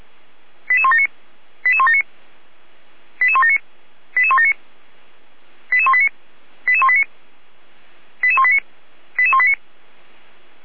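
Five-tone selective call (ZVEI), heard as quick runs of five tones, tones of about 70 ms. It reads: 88179 88179 88179 88179 88189 88189 88189 88189
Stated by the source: Five-tone ZVEI selective call received on an SDR, from the feuerwehr-orlamuende.org fire brigade audio server.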